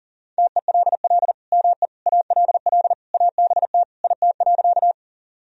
Morse code, heard as a keyed tone, about 37 wpm, 700 Hz. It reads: TELL ME ALL ABT IT.
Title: Sentences from Top 200 Words and Abbreviations 40wpm